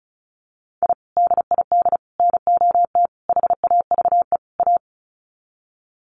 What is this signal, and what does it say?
Morse code, heard as a keyed tone, about 34 wpm, 700 Hz. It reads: I DID NOT HAVE A
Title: Sets of 5 Words 35wpm